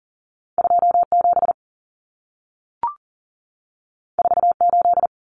Morse code, heard as a keyed tone, about 40 wpm, 700 Hz. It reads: 27 48
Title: Common Cut Numbers - 2-Digit 40wpm